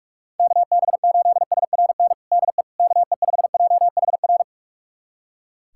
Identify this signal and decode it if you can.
Morse code, heard as a keyed tone, about 46 wpm, 700 Hz. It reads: KB8SRN DE KE5JHR
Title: USA Callsign DE Callsign - 6 Characters - 45wpm